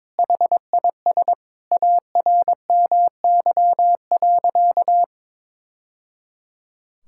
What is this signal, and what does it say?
Morse code, heard as a keyed tone, about 22 wpm, 700 Hz. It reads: HIS ARMY.
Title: Sentences from Top 1000 Words - Review Part 6 - 22wpm